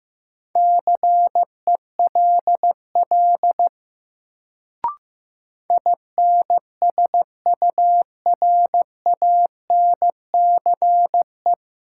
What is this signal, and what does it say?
Morse code, heard as a keyed tone, about 15 wpm, 700 Hz. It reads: CELL INSURANCE
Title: Sentences from Top 1100 Words - New Word Drill - 15wpm